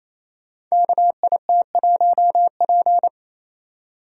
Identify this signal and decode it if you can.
Morse code, heard as a keyed tone, about 29 wpm, 700 Hz. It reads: KIT1P
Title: Code Groups - 5 Letter:Numbers - 28wpm